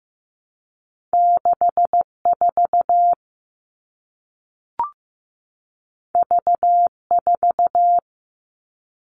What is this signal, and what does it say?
Morse code, heard as a keyed tone, about 15 wpm, 700 Hz. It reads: 64 V4
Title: All Cut Numbers - 2-Digit 15wpm